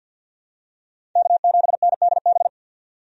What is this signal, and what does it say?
Morse code, heard as a keyed tone, about 53 wpm, 700 Hz. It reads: K7NDB
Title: USA Call Signs - 5 Characters - 50wpm